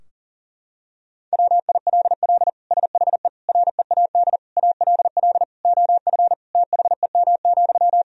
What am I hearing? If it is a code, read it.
Morse code, heard as a keyed tone, about 40 wpm, 700 Hz. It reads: WILL SHE READ ALL OF THEM,
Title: Sentences from Top 1000 Words - Review Part 4 - 40wpm